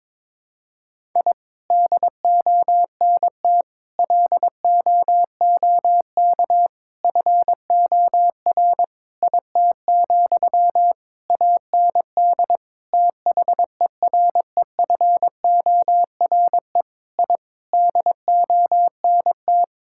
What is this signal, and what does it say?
Morse code, heard as a keyed tone, about 22 wpm, 700 Hz. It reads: I DONT LOOK FOR IT, AND THEREFORE I DONT
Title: Sentences from Top 1000 Words - Review Part 5 - 22wpm